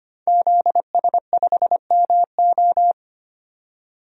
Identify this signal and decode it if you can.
Morse code, heard as a keyed tone, about 25 wpm, 700 Hz. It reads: ZS5MO
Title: DXCC-100 Call Signs - 5 Characters - 25wpm